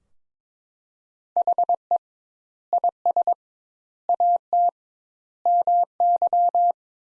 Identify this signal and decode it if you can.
Morse code, heard as a keyed tone, about 22 wpm, 700 Hz. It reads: HE IS AT MY
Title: Sentences Top 200 Words - 22wpm